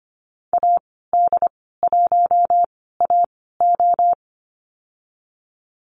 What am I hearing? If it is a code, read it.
Morse code, heard as a keyed tone, about 25 wpm, spaced as for 16 wpm, 700 Hz. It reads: AD1AO
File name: Lesson 15g - Number 5 - Call Signs 25wpm